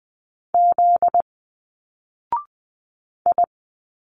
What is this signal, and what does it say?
Morse code, heard as a keyed tone, about 20 wpm, 700 Hz. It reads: Z I
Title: Single Letters 20wpm